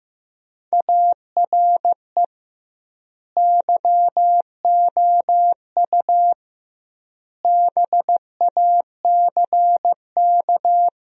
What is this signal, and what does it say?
Morse code, heard as a keyed tone, about 15 wpm, 700 Hz. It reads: ARE YOU BACK
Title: 2x Word-Spacing - Sentences from Top 200 Words - 15wpm